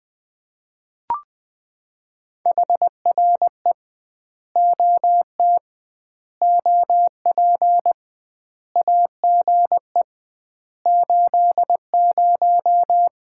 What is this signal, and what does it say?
Morse code, heard as a keyed tone, about 20 wpm, 700 Hz. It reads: HRE OT OP AGE 80
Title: Common QSO Phrases - 2x Word-Spacing - 20wpm